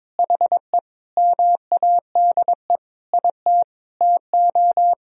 Morse code, heard as a keyed tone, about 22 wpm, 700 Hz. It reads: HE MADE IT TO
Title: Sentences from Top 1000 Words - Review Part 2 - 22wpm